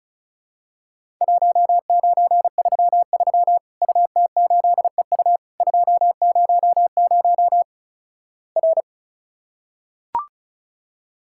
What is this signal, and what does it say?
Morse code, heard as a keyed tone, about 36 wpm, 700 Hz. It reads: 1933 UT8EU 200 R
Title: Worked All Europe DX Contest - QTC Exchange - 35wpm